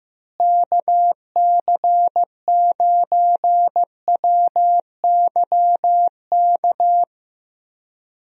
Morse code, heard as a keyed tone, about 15 wpm, 700 Hz. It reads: KC9WYK
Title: USA Call Signs - 6 Characters - Repeated 2x - 15wpm